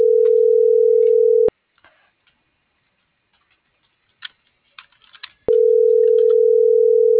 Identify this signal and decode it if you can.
Ringback tone (North America), 440 + 480 Hz.